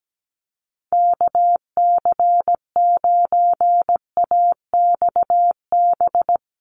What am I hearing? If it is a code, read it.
Morse code, heard as a keyed tone, about 17 wpm, 700 Hz. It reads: KC9AXB
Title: Common QSO Phrases Encore 17wpm